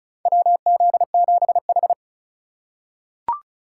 Morse code, heard as a keyed tone, about 36 wpm, 690 Hz. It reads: WZ7H
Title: USA Call Signs - 4 Characters - 2x Character Spacing - 35wpm